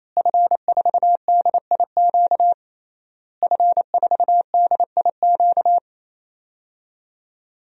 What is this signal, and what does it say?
Morse code, heard as a keyed tone, about 28 wpm, 700 Hz. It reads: F4DIQ F4DIQ
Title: DXCC-100 Call Signs - 5 Characters - Repeated 3x - 28wpm